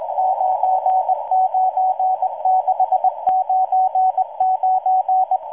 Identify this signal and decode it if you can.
Morse code, heard as a keyed tone, about 19 wpm, 750 Hz. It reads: 99699